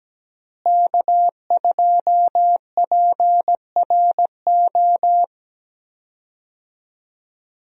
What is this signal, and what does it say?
Morse code, heard as a keyed tone, about 17 wpm, 700 Hz. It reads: K2PRO
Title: USA Call Signs - 5 Characters - Repeated 3x - 17wpm